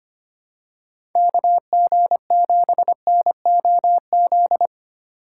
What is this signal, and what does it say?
Morse code, heard as a keyed tone, about 25 wpm, 700 Hz. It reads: KG7NOZ
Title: USA Call Signs - 6 Characters - 4x Character Spacing - 25wpm